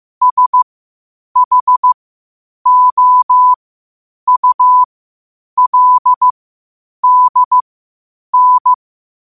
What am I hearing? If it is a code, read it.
Morse code, heard as a keyed tone, about 15 wpm, spaced as for 8 wpm, 1000 Hz. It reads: SHOULDN
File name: qotdfiles / 8wpm